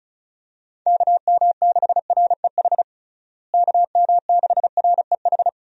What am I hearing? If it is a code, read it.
Morse code, heard as a keyed tone, about 34 wpm, 700 Hz. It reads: KM6REH KM6REH
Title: USA Call Signs - 6 Characters - Repeated 2x - 35wpm